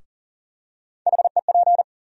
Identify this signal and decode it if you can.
Morse code, heard as a keyed tone, about 40 wpm, 700 Hz. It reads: HEP
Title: USA Call Sign Postfixes - 3 Letters 40wpm